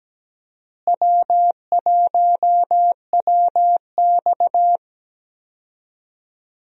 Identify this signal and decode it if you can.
Morse code, heard as a keyed tone, about 17 wpm, 700 Hz. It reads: W1WX
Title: USA Call Signs - 4 Characters - Repeated 3x - 17wpm